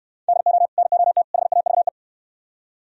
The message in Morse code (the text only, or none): DO NOT HAVE